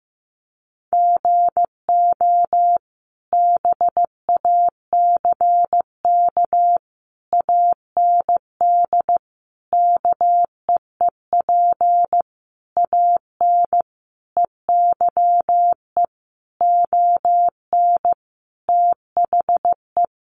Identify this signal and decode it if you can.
Morse code, heard as a keyed tone, about 15 wpm, 700 Hz. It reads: GO BACK AND KEEP AN EYE ON THE
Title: Sentences from Top 300 Words 15wpm